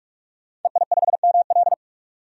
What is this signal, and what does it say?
Morse code, heard as a keyed tone, about 47 wpm, 700 Hz. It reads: EI5ML
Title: DXCC-100 Call Signs - 5 Characters - 45wpm